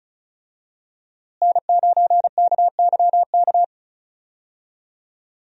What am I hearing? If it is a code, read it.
Morse code, heard as a keyed tone, about 34 wpm, 700 Hz. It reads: N9KYK
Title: USA Call Signs - 5 Characters - Repeated 2x - 35wpm